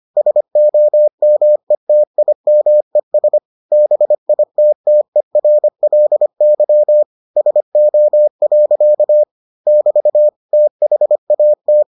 Morse code, heard as a keyed tone, about 25 wpm, 600 Hz. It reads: SOMETIMES BITTERLY SO. <BT> THAT